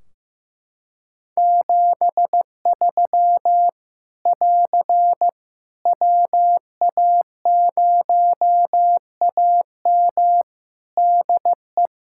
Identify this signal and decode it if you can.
Morse code, heard as a keyed tone, about 15 wpm, 700 Hz. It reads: 73 <AR> WA0AM DE
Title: K1USN SST QSO Practice 15wpm